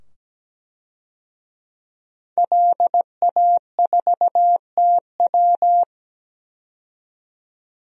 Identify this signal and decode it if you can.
Morse code, heard as a keyed tone, about 17 wpm, 700 Hz. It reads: LA4TW
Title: DXCC-100 Call Signs - 5 Characters - 17wpm